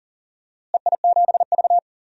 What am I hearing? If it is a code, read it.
Morse code, heard as a keyed tone, about 40 wpm, 700 Hz. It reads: EI7V
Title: DXCC-100 Call Signs - 4 Characters - Repeated 3x - 40wpm